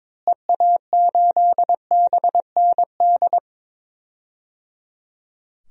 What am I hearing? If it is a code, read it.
Morse code, heard as a keyed tone, about 22 wpm, 700 Hz. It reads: EA8BND
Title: DXCC-100 Call Signs - 6 Characters - Repeated 3x - 22wpm